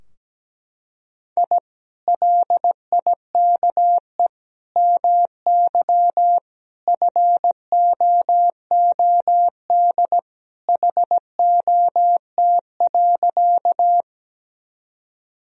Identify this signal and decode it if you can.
Morse code, heard as a keyed tone, about 17 wpm, 700 Hz. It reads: I LIKE MY FOOD HOT.
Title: Sentences from Top 1000 Words - Review Part 3 - 17wpm